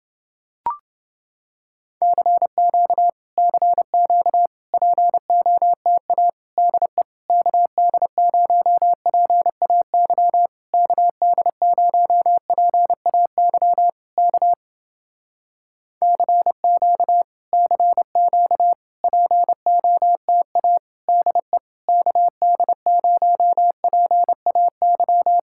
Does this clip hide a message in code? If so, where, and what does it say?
Yes: Morse code from 2.0 to 25.4 s, heard as a keyed tone, about 30 wpm, 700 Hz. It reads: CQ CQ POTA DE KD0PAY KD0PAY K CQ CQ POTA DE KD0PAY